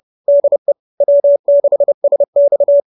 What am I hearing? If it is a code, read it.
Morse code, heard as a keyed tone, about 30 wpm, 570 Hz. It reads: DE W6SX